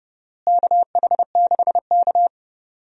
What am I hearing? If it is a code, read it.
Morse code, heard as a keyed tone, about 30 wpm, 700 Hz. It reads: KH6K
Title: DXCC-100 Call Signs - 4 Characters - Repeated 3x - 30wpm